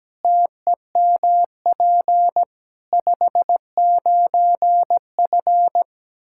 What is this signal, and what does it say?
Morse code, heard as a keyed tone, about 17 wpm, 700 Hz. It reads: TEMP 59F
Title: Practice QSO - Exchange 2 - Info - 17wpm